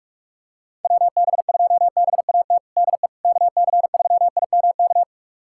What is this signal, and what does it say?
Morse code, heard as a keyed tone, about 46 wpm, 700 Hz. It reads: WB2BAT DE KC3IMK
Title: USA Callsign DE Callsign - 6 Characters - 45wpm